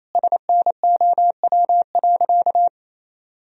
Morse code, heard as a keyed tone, about 28 wpm, 700 Hz. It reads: SNOW.
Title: Sentences from Top 1000 Words - Review Part 3 - 28wpm